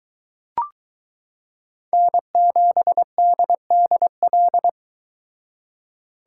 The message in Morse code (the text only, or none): N7DDL